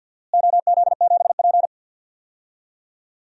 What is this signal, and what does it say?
Morse code, heard as a keyed tone, about 47 wpm, 700 Hz. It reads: OZ7P
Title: DXCC-100 Call Signs - 4 Characters - Repeated 2x - 50wpm